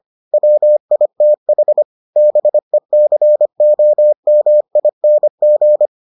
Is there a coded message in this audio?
Yes: Morse code from 0.3 to 5.9 s, heard as a keyed tone, about 25 wpm, 600 Hz. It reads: WITH BECOMING